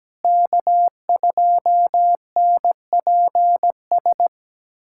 Morse code, heard as a keyed tone, about 17 wpm, 700 Hz. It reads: K2NPS
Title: USA Callsign DE Callsign 17wpm